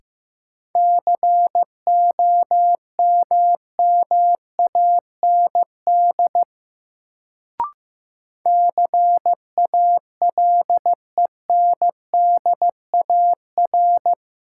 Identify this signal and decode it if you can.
Morse code, heard as a keyed tone, about 15 wpm, 700 Hz. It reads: COMMAND CALENDAR
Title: Sentences from Top 1600 Words - New Word Drill - 15wpm